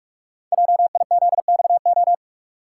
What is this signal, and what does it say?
Morse code, heard as a keyed tone, about 44 wpm, 700 Hz. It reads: JIZXO
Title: Code Groups - 5 Letters - 45wpm